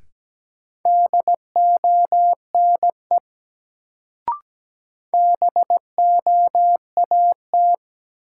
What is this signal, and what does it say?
Morse code, heard as a keyed tone, about 17 wpm, 700 Hz. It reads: DONE BOAT
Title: Sentences from Top 500 Words - New Word Drill - 17wpm